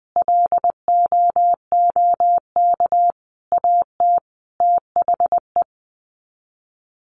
Morse code, heard as a keyed tone, about 20 wpm, 700 Hz. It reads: LOOK AT THE
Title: Sets of 5 Words 20wpm